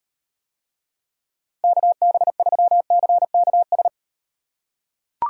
Morse code, heard as a keyed tone, about 38 wpm, 700 Hz. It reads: KB3CKS